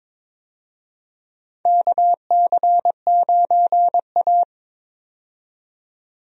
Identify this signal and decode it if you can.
Morse code, heard as a keyed tone, about 22 wpm, 700 Hz. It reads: KC9A
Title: USA Call Signs - 4 Characters - 22wpm